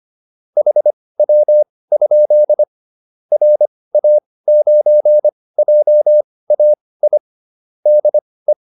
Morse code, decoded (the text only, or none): HW? RA9JAI DE